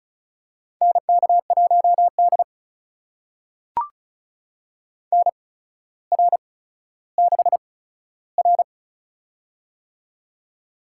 Morse code, heard as a keyed tone, about 36 wpm, spaced as for 15 wpm, 700 Hz. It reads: NK1D NR6R
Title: USA Call Signs - 8x Character Spacing - 35wpm